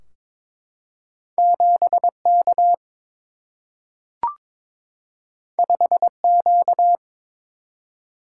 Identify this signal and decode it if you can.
Morse code, heard as a keyed tone, about 22 wpm, 700 Hz. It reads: 7 K 5 Q
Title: Single-Letter Single-Number 22wpm